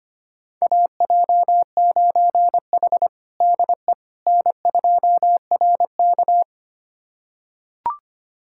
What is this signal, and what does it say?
Morse code, heard as a keyed tone, about 25 wpm, 700 Hz. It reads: AJ9H DE N2RK
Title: USA Callsign DE Callsign - 4 Characters - 25wpm